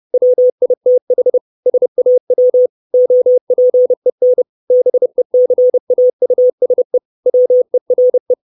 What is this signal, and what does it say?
Morse code, heard as a keyed tone, about 30 wpm, 500 Hz. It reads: WITH SAW OPEN BECAUSE WERE
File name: 632 Words 30WPM